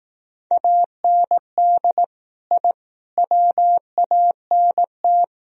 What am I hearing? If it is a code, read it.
Morse code, heard as a keyed tone, about 17 wpm, 700 Hz. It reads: AND I WANT
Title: Speed-Racing - Sets of 4 Words 15wpm